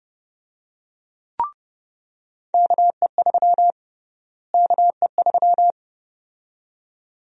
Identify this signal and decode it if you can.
Morse code, heard as a keyed tone, about 30 wpm, 700 Hz. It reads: KE3 KE3
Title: USA Call Sign Prefixes - Repeated 2x - 30wpm